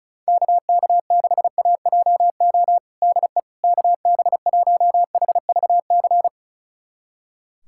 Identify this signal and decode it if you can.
Morse code, heard as a keyed tone, about 34 wpm, 690 Hz. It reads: KK6AJO DE KB1HVC
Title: USA Callsign DE Callsign 35wpm